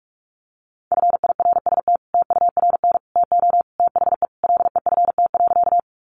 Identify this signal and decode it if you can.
Morse code, heard as a keyed tone, about 44 wpm, 700 Hz. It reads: FIRST TURN TO THE LEFT.
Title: Sentences from Top 300 Words 45wpm